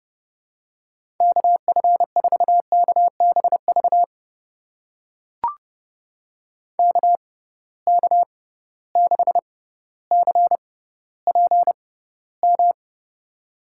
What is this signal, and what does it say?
Morse code, heard as a keyed tone, about 30 wpm, spaced as for 15 wpm, 700 Hz. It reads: KF4KBV KK6CPM
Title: USA Call Signs - 6 Characters - 6x Character Spacing - 30wpm